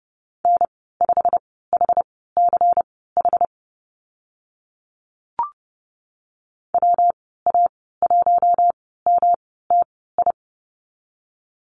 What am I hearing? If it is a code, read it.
Morse code, heard as a keyed tone, about 30 wpm, spaced as for 17 wpm, 700 Hz. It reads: N5HCH WA1MTI
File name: Lesson 19g - Letter w - Call Signs 30wpm